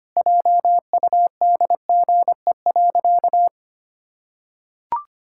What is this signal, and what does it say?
Morse code, heard as a keyed tone, about 25 wpm, 700 Hz. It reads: JUDGE.
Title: Pangram Challenge v2 - 25wpm